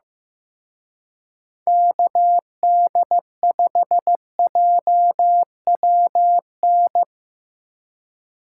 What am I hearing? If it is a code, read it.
Morse code, heard as a keyed tone, about 17 wpm, 700 Hz. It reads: KD5JWN
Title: Speed Racing - USA 2x3 Call Signs 15wpm